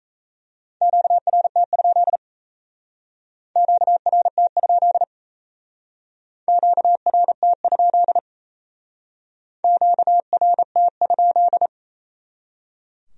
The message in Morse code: QRT? QRT? QRT? QRT?

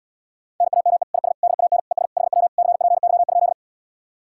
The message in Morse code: NAME IS DAN ES HW CPY?

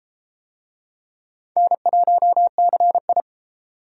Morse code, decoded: N1CI

